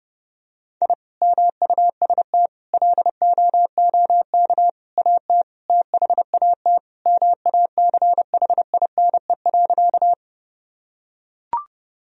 Morse code, heard as a keyed tone, about 30 wpm, 700 Hz. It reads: I MUST LOOK AT THAT MACHINE.
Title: Sentences from Top 1000 Words - Review Part 2 - 30wpm